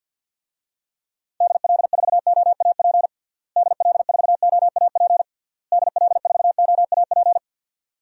Morse code, heard as a keyed tone, about 53 wpm, 700 Hz. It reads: DL4OAP DL4OAP DL4OAP
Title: USA Call Signs - 6 Characters - Repeated 3x - 50wpm